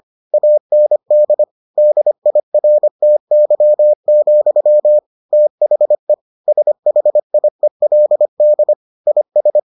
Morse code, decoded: AND DIRTY, THE SHIELD IS

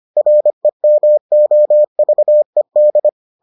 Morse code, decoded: REMOVED

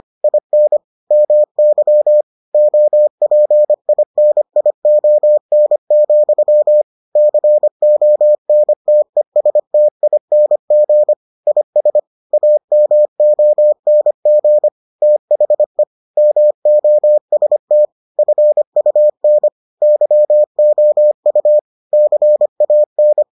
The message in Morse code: IN MY OPINION, CONTESTING IS AMONG THE MOST FUN YOU CAN

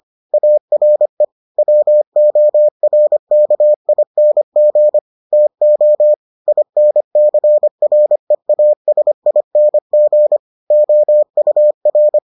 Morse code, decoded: ARE WORKING TO INCREASING OUR

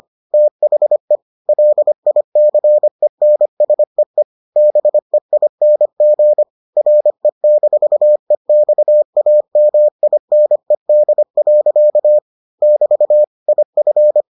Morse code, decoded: THE LICENSEE BEING RE-EXAMINED. <BT> IF

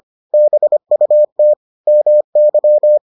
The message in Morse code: BUT MY

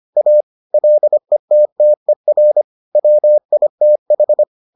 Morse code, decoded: A LETTER WITH

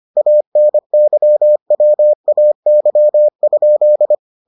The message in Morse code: ANYWAY?

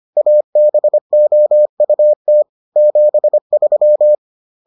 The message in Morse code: ABOUT 73

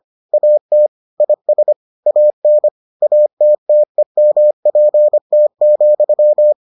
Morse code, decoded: AT IS AN ATTEMPT,

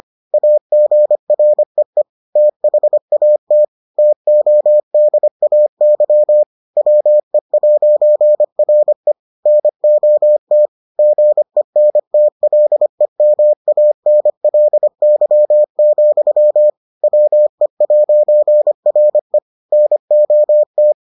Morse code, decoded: AGREE THAT TODAY WE'RE NOT GENTLEMANLY, WE'RE NOT

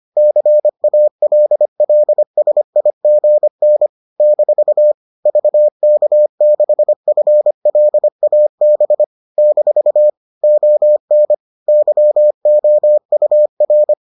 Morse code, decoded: CALLSIGN - VK6FLAB - ON YOUR

